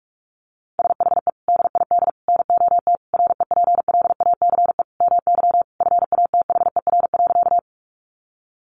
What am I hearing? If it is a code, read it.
Morse code, heard as a keyed tone, about 44 wpm, 700 Hz. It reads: SHE DID NOT REPLACE MY FATHER.
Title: Sentences from Top 1400 Words 45wpm